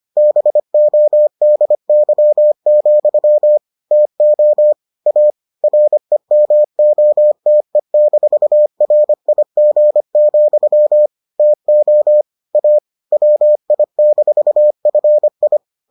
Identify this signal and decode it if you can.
Morse code, heard as a keyed tone, about 25 wpm, 600 Hz. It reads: BODY, TO A REMOTE-RIG, TO A WI-FI